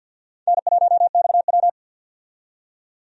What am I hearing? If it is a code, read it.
Morse code, heard as a keyed tone, about 48 wpm, 700 Hz. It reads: N1XW
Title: USA Call Signs - 4 Characters - 50wpm